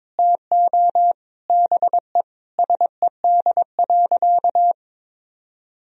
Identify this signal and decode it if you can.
Morse code, heard as a keyed tone, about 22 wpm, 700 Hz. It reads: TO BE SED.
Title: Sentences from Top 100 Words and Abbreviations 22wpm